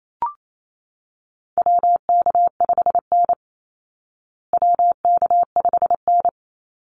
Morse code, heard as a keyed tone, about 28 wpm, 700 Hz. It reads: WK5N WK5N
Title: Call Signs Repeated 3x - 28wpm